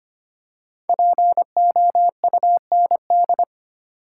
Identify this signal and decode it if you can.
Morse code, heard as a keyed tone, about 25 wpm, 700 Hz. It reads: POUND
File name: 5 Letter Words - First 3 Letters Given - 25wpm